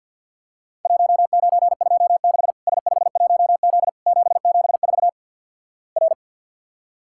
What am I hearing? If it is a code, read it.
Morse code, heard as a keyed tone, about 50 wpm, 700 Hz. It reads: 1926 S51Z 774 R
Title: Worked All Europe DX Contest - QTC Exchange with Repeats - 50wpm